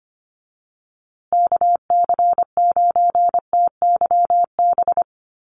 Morse code, heard as a keyed tone, about 24 wpm, 700 Hz. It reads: KC9TYB
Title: USA Call Signs Repeated 3x 25wpm